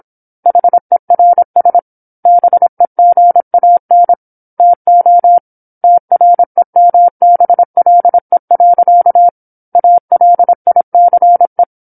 Morse code, heard as a keyed tone, about 27 wpm, 700 Hz. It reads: HERS BEGAN TO TREMBLE. ALICE
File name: 08-Chapter-08 26 WPM